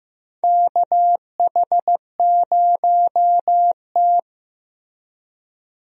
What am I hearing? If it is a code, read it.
Morse code, heard as a keyed tone, about 15 wpm, 700 Hz. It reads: KH0T
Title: USA Call Signs - 4 Characters - Repeated 3x - 15wpm